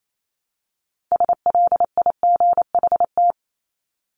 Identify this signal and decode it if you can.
Morse code, heard as a keyed tone, about 28 wpm, 700 Hz. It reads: SLIGHT
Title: LY Suffix - Top 100 Words with Base Words 28wpm